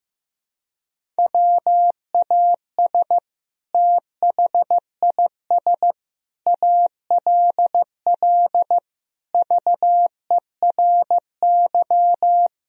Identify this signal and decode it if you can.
Morse code, heard as a keyed tone, about 16 wpm, 700 Hz. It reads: WAS THIS ALL VERY
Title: Sentences from Top 1000 Words - Review Part 3 - 15wpm